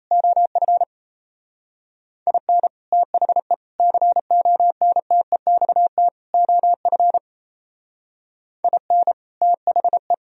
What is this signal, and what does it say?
Morse code, heard as a keyed tone, about 32 wpm, 700 Hz. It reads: OF IN THE CONTEXT OF IN THE
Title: Speed-Racing - Sets of 4 Words 28wpm